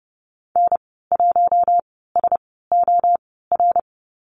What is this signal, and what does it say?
Morse code, heard as a keyed tone, about 30 wpm, spaced as for 17 wpm, 700 Hz. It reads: N1SOR
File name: Lesson 15g - Number 5 - Call Signs 30wpm